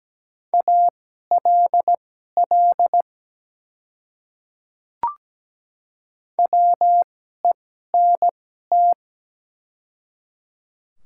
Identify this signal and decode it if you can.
Morse code, heard as a keyed tone, about 17 wpm, spaced as for 12 wpm, 700 Hz. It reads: ALL WENT
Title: 2x Character Spacing - Top 300 Words - 17wpm